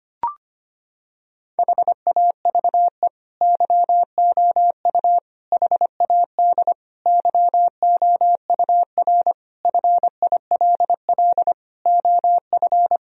Morse code, heard as a keyed tone, about 25 wpm, 700 Hz. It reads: HAVE YOU HAD YOUR FILL OF
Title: Sentences from Top 1000 Words - Review Part 3 - 25wpm